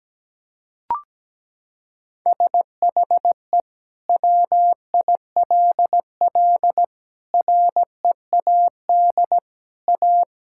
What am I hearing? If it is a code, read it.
Morse code, heard as a keyed tone, about 17 wpm, 700 Hz. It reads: SHE WILL READ A
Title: Sentences from Top 1000 Words - Review Part 6 - 17wpm